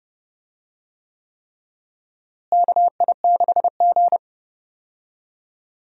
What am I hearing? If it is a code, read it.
Morse code, heard as a keyed tone, about 30 wpm, 700 Hz. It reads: KI6G
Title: USA Call Signs - 4 Characters - Repeated 3x - 30wpm